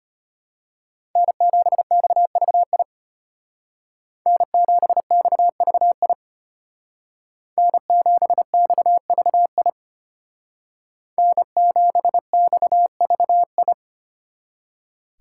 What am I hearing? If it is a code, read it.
Morse code, heard as a keyed tone, about 30 wpm, 700 Hz. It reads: N7XVI N7XVI N7XVI N7XVI
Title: Speed Racing - USA 1x3 Call Signs 25wpm